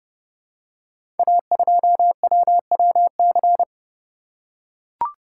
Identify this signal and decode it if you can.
Morse code, heard as a keyed tone, about 30 wpm, 700 Hz. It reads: A2WWC
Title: Code Groups - 5 Letter:Numbers - 30wpm